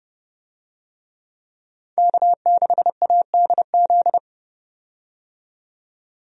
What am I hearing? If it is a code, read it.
Morse code, heard as a keyed tone, about 30 wpm, 700 Hz. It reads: K6ADZ
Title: USA Call Signs - 5 Characters - 30wpm